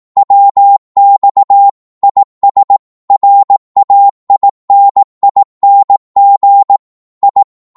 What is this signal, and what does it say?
Morse code, heard as a keyed tone, about 18 wpm, 800 Hz. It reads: WX IS RAINING I
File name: Morse / qso-18wpm